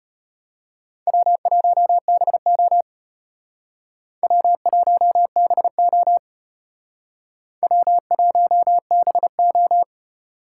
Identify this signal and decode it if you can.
Morse code, heard as a keyed tone, about 32 wpm, 700 Hz. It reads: W1BO W1BO W1BO